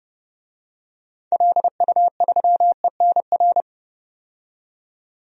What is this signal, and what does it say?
Morse code, heard as a keyed tone, about 30 wpm, 700 Hz. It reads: LU3ENR
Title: DXCC-100 Call Signs - 6 Characters - 30wpm